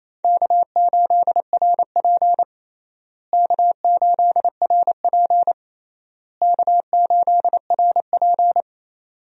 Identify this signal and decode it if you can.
Morse code, heard as a keyed tone, about 28 wpm, 700 Hz. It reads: K8RP K8RP K8RP
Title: USA Call Signs - 4 Characters - Repeated 3x - 28wpm